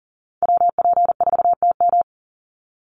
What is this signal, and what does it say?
Morse code, heard as a keyed tone, about 40 wpm, 700 Hz. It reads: WP4TM
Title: USA Call Signs 40wpm